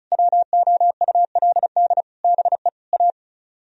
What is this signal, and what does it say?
Morse code, heard as a keyed tone, about 47 wpm, spaced as for 41 wpm, 700 Hz. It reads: WOULD BE A
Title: Speed-Racing - Sets of 4 Words 35wpm